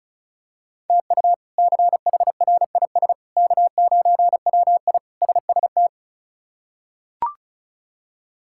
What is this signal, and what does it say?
Morse code, heard as a keyed tone, about 36 wpm, 700 Hz. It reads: TU CHRIS K9WI SST
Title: K1USN SST QSO Practice - 35wpm